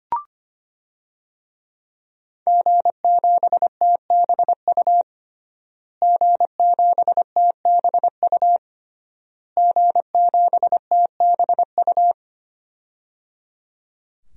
Morse code, heard as a keyed tone, about 25 wpm, 700 Hz. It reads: G7TBU G7TBU G7TBU
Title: DXCC-100 Call Signs - 5 Characters - Repeated 3x - 25wpm